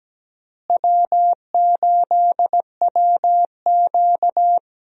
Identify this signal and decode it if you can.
Morse code, heard as a keyed tone, about 17 wpm, 700 Hz. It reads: W8WQ
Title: USA Call Signs - 4 Characters - Repeated 2x - 17wpm